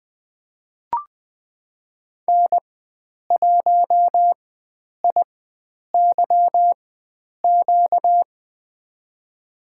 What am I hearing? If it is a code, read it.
Morse code, heard as a keyed tone, about 20 wpm, spaced as for 9 wpm, 700 Hz. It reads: N1IYQ
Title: USA Call Signs - 5 Characters - 4x Character Spacing - 20wpm